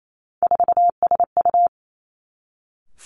Morse code, heard as a keyed tone, about 28 wpm, 700 Hz. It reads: 4SU
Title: Three Challenging Letter-Number Combinations 28wpm